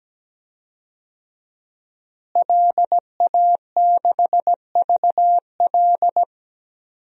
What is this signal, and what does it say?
Morse code, heard as a keyed tone, about 17 wpm, 700 Hz. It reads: LA6VL
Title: DXCC-100 Call Signs - 5 Characters - Repeated 3x - 17wpm